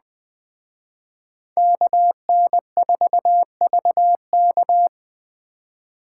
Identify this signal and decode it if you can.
Morse code, heard as a keyed tone, about 20 wpm, 700 Hz. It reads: KN4VK
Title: Speed Racing - USA 2x2 Call Signs 17wpm